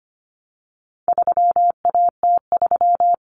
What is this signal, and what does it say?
Morse code, heard as a keyed tone, about 25 wpm, 700 Hz. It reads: 3AT3
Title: Common Cut Numbers - 4-Digit 25wpm